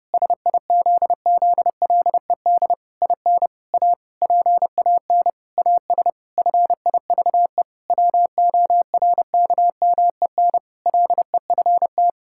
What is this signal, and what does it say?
Morse code, heard as a keyed tone, about 32 wpm, 700 Hz. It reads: SIZZLED IN A PAN AS FIVE WORKMEN LEFT
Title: Pangram Challenge v2 - 30wpm